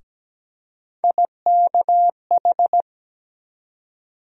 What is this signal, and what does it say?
Morse code, heard as a keyed tone, about 17 wpm, 700 Hz. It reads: IKH
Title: USA Call Sign Postfixes - 3 Letters 17wpm